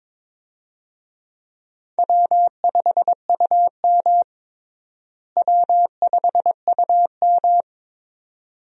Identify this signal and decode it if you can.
Morse code, heard as a keyed tone, about 22 wpm, 700 Hz. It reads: W5UM W5UM
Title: USA Call Signs - 4 Characters - Repeated 2x - 22wpm